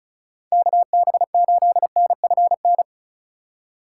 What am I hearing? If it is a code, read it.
Morse code, heard as a keyed tone, about 33 wpm, 700 Hz. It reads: KB8NFN
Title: Speed Racing - USA 2x3 Call Signs 35wpm